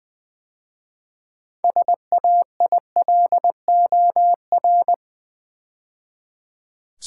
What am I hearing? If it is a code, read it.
Morse code, heard as a keyed tone, about 20 wpm, 700 Hz. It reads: SAILOR
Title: 6 Letter Words - First 3 Letters Given - 20wpm